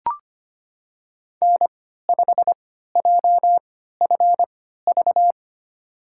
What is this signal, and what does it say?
Morse code, heard as a keyed tone, about 25 wpm, spaced as for 14 wpm, 700 Hz. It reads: N5JFV
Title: USA Call Signs - 5 Characters - 3x Character Spacing - 25wpm